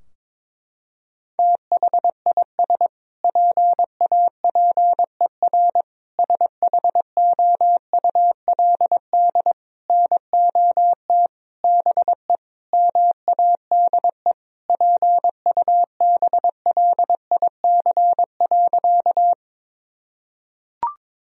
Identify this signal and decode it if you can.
Morse code, heard as a keyed tone, about 21 wpm, 700 Hz. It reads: THIS PAPER SHOULD NOT BE MADE PUBLIC.
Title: Sentences from Top 1000 Words - Review Part 3 - 22wpm